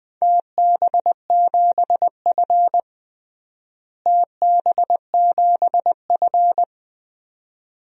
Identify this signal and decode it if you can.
Morse code, heard as a keyed tone, about 20 wpm, 700 Hz. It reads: TB7F TB7F
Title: DXCC-100 Call Signs - 4 Characters - Repeated 2x - 20wpm